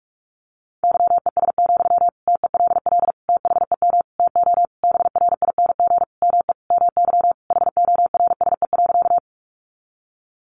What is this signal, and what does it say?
Morse code, heard as a keyed tone, about 44 wpm, 700 Hz. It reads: YES, TELL THEM TO BRING ME MY HORSE.
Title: Sentences from Top 500 Words 45wpm